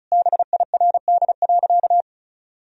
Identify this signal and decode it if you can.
Morse code, heard as a keyed tone, about 36 wpm, 700 Hz. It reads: BIRD.
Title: Sentences from Top 1000 Words - Review Part 2 - 35wpm